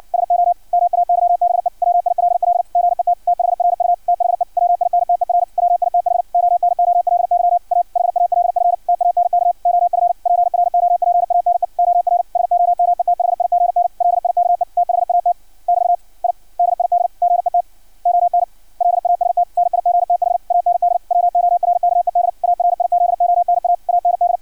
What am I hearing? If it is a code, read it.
Morse code, heard as a keyed tone, about 56 wpm, 700 Hz. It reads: SO MAYBE WILL GET THRU THE WINTER WITH ONOLY A HALF TANK OF PROPANE OR SOMETHIGN LIKE THAT <BT> I DID GET ON LAST NIGTH AND WORKED ARIZONA AND
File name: archived / CRSnet